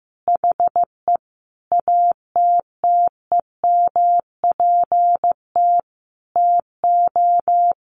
Morse code, heard as a keyed tone, about 15 wpm, 700 Hz. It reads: HE ATTEMPT TO
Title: Sentences from Top 1400 Words 15wpm